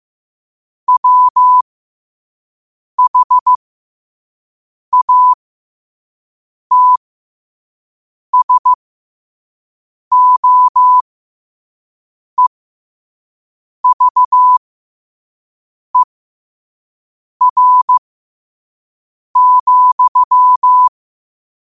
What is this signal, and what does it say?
Morse code, heard as a keyed tone, about 15 wpm, spaced as for 5 wpm, 1000 Hz. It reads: WHATSOEVER,
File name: qotdfiles / 5wpm